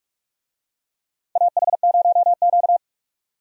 Morse code, heard as a keyed tone, about 46 wpm, 700 Hz. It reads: AH0Q
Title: USA Call Signs - 4 Characters - 45wpm